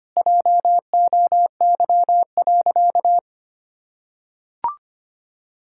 Morse code, heard as a keyed tone, about 25 wpm, 700 Hz. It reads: JOY.
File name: practice / 25wpm / Difficult Sentences with Hints - 25wpm